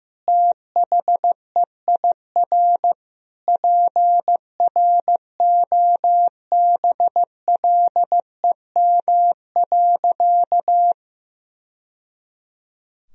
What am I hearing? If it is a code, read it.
Morse code, heard as a keyed tone, about 16 wpm, 700 Hz. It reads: THEIR PROBLEM.
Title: Sentences from Top 1000 Words - Review Part 2 - 15wpm